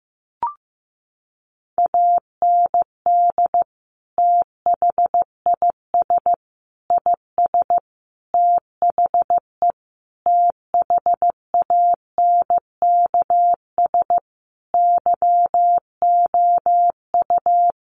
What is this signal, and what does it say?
Morse code, heard as a keyed tone, about 15 wpm, 700 Hz. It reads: AND THIS IS THE THANKS YOU
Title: Sentences from Top 1200 Words 15wpm